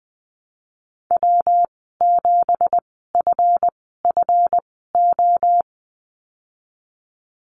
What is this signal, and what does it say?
Morse code, heard as a keyed tone, about 20 wpm, spaced as for 15 wpm, 700 Hz. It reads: W7FFO
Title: 179 Lesson 29g - Slash - Call Signs 20wpm